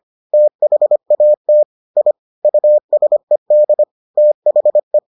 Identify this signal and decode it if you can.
Morse code, heard as a keyed tone, about 25 wpm, 600 Hz. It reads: THAT I USED THE